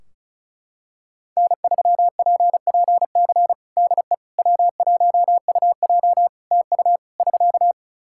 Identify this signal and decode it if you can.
Morse code, heard as a keyed tone, about 35 wpm, 700 Hz. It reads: N3PPC DE W1UJ TU <SK>
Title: Practice QSO - Exchange 4 - Ending - 35wpm